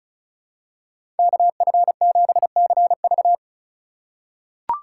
Morse code, heard as a keyed tone, about 34 wpm, 700 Hz. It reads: KF7CV
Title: USA Call Signs - 5 Characters - 3x Character Spacing - 35wpm